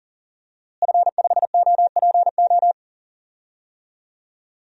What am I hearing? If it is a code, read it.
Morse code, heard as a keyed tone, about 40 wpm, 700 Hz. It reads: F5OPO
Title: DXCC-100 Call Signs - 5 Characters - Repeated 3x - 40wpm